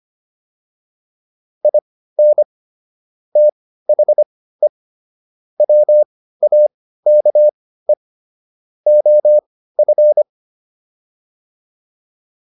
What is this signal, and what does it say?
Morse code, heard as a keyed tone, about 25 wpm, spaced as for 15 wpm, 600 Hz. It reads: IN THE WAKE OF